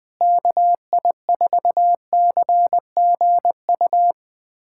Morse code, heard as a keyed tone, about 21 wpm, 700 Hz. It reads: KI4CGU
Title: USA Callsign DE Callsign - 6 Characters - 20wpm